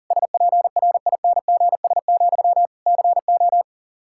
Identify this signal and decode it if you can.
Morse code, heard as a keyed tone, about 40 wpm, 700 Hz. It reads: SPRINGS, CO
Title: Practice QSO - Exchange 1 - Beginning - 40wpm